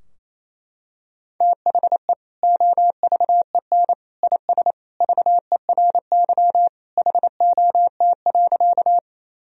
Morse code, heard as a keyed tone, about 28 wpm, 700 Hz. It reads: THE OVEN IS VERY HOT.
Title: Sentences from Top 1000 Words - Review Part 4 - 28wpm